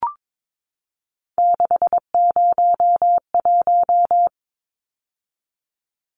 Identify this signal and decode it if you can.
Morse code, heard as a keyed tone, about 22 wpm, 700 Hz. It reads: 601
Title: Numbers - 3-Digit 22wpm